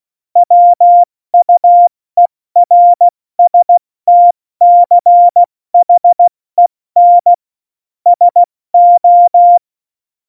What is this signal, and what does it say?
Morse code, heard as a keyed tone, about 16 wpm, 700 Hz. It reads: WUERSTCHEN SO
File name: Morsekurs / Witze / Witz 06 March 2018, 70 F80